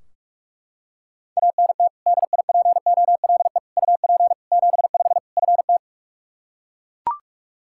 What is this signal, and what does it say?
Morse code, heard as a keyed tone, about 42 wpm, 700 Hz. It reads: ANT DIPOLE UP 75 FT
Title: Practice QSO - Exchange 2 - Info - 45wpm